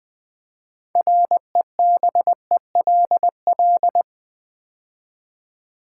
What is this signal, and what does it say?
Morse code, heard as a keyed tone, about 20 wpm, 700 Hz. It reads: REBELL
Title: Build A Word - 3 Letters At A Time - 9-Letter Words - 20wpm